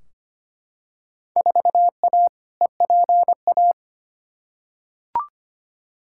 Morse code, heard as a keyed tone, about 25 wpm, 700 Hz. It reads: 4A EPA
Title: Field Day Exchanges - 25wpm